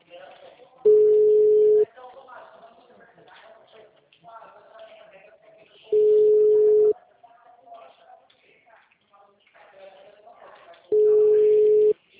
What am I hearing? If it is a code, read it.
Ringback tone (Europe, 425 Hz), with speech.